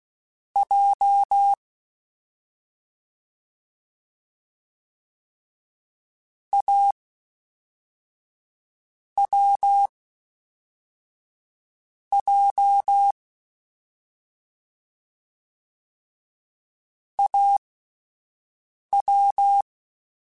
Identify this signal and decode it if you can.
Morse code, heard as a keyed tone, about 16 wpm, 800 Hz. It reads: J A W J A W
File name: Difficult Combos / Difficult Combos-mp3s / AWJ ShortRun @10